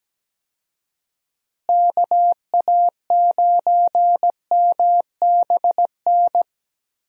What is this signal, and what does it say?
Morse code, heard as a keyed tone, about 17 wpm, 700 Hz. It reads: KA9MBN
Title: USA Call Signs - 6 Characters - Repeated 2x - 17wpm